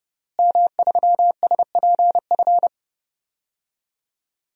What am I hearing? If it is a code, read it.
Morse code, heard as a keyed tone, about 30 wpm, 700 Hz. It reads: M3SPF
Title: DXCC-100 Call Signs - 5 Characters - Repeated 3x - 30wpm